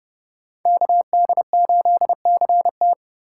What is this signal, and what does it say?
Morse code, heard as a keyed tone, about 30 wpm, 700 Hz. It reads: KD8CT